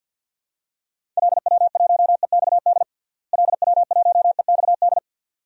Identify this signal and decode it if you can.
Morse code, heard as a keyed tone, about 46 wpm, 700 Hz. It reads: LW1EXD LW1EXD
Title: DXCC-100 Call Signs - 6 Characters - Repeated 2x - 50wpm